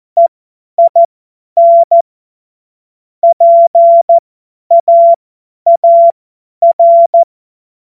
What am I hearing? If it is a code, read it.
Morse code, heard as a keyed tone, about 14 wpm, spaced as for 10 wpm, 680 Hz. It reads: EIN PAAR
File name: Morsekurs / Witze / Witz 25. Oktober 2022 50 F70